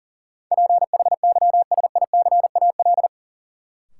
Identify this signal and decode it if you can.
Morse code, heard as a keyed tone, about 40 wpm, 700 Hz. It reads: PHYSICAL
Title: Sentences from Top 1400 Words - New Word Drill Alt - 40wpm